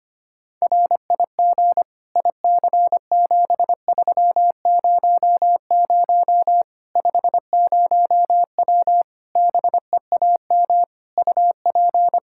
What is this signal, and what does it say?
Morse code, heard as a keyed tone, about 25 wpm, 700 Hz. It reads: RIG IC7300 50W BEAM UP
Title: Practice QSO - Exchange 2 - Info - 25wpm